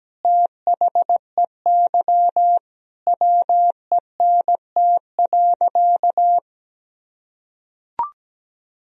Morse code, heard as a keyed tone, about 17 wpm, 700 Hz. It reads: THEY WENT.
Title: Sentences from Top 1000 Words - Review Part 3 - 17wpm